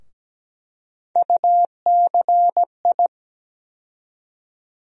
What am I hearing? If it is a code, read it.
Morse code, heard as a keyed tone, about 17 wpm, 700 Hz. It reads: UCI
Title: USA Call Sign Postfixes - 3 Letters 17wpm